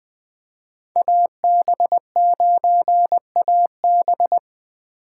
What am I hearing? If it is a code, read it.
Morse code, heard as a keyed tone, about 20 wpm, 700 Hz. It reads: AB9AB
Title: USA Call Signs - 5 Characters - 3x Character Spacing - 20wpm